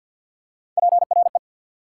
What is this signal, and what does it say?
Morse code, heard as a keyed tone, about 48 wpm, 700 Hz. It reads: PRE
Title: Prefixes and Suffixes 50wpm